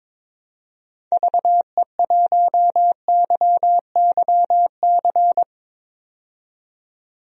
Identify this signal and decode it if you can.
Morse code, heard as a keyed tone, about 22 wpm, 700 Hz. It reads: VE1YYC
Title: DXCC-100 Call Signs - 6 Characters - 22wpm